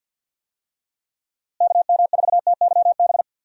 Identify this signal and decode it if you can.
Morse code, heard as a keyed tone, about 51 wpm, 700 Hz. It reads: KM4TYB
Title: USA Call Signs - 6 Characters - Repeated 2x - 50wpm